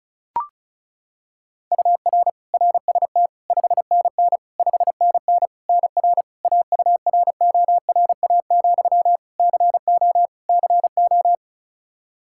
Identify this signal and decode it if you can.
Morse code, heard as a keyed tone, about 36 wpm, 700 Hz. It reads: UR RST 5NN 5NN NR AURORA, CO CO
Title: Practice QSO - Exchange 1 - Beginning - 35wpm